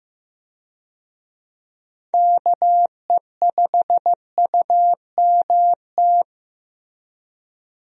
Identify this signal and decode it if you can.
Morse code, heard as a keyed tone, about 15 wpm, 700 Hz. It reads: KE5UMT
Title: USA Call Signs - 6 Characters - Repeated 2x - 15wpm